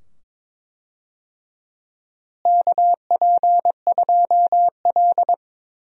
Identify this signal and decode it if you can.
Morse code, heard as a keyed tone, about 21 wpm, 700 Hz. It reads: KP2L
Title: DXCC-100 Call Signs - 4 Characters - Repeated 2x - 22wpm